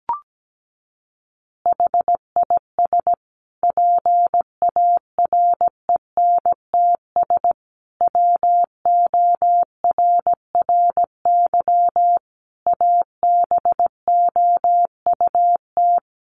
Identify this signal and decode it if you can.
Morse code, heard as a keyed tone, about 17 wpm, 700 Hz. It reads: HIS PARENTS WORRY ABOUT
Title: Easy Phrases 17wpm